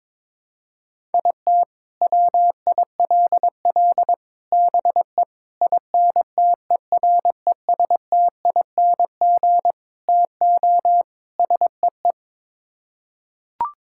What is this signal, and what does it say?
Morse code, heard as a keyed tone, about 22 wpm, 700 Hz. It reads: IT WILL BE INTERESTING TO SEE
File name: practice / 22wpm / Sets of 6 Words - 22wpm